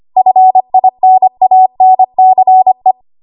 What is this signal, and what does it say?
Morse code, heard as a keyed tone, about 25 wpm, 750 Hz. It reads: FINANCE
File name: Archive / 25 WPM / 25WPM